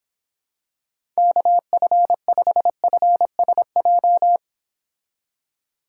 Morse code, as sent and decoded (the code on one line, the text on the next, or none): -.- ..-. ..... ..-. ... .---
KF5FSJ